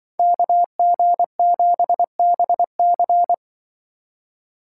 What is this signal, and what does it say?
Morse code, heard as a keyed tone, about 24 wpm, 700 Hz. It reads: KG7BC
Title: Speed Racing - USA 2x2 Call Signs 20wpm